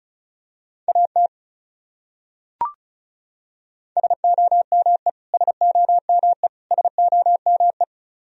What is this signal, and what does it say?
Morse code, heard as a keyed tone, about 36 wpm, 700 Hz. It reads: AT SOME SOME SOME
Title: Top 100 Words Repeated 35wpm